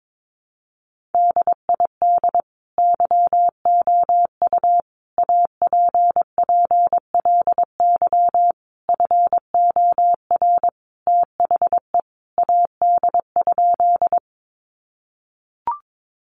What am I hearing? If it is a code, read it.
Morse code, heard as a keyed tone, about 22 wpm, 700 Hz. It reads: DID YOU APPLY FOR THE AD?
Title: Sentences from Top 1300 Words 22wpm